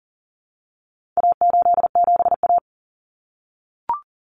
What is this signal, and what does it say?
Morse code, heard as a keyed tone, about 40 wpm, 700 Hz. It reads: A87A
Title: Common Cut Numbers - 4-Digit 40wpm